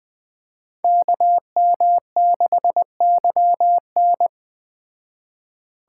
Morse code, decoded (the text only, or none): KM6YN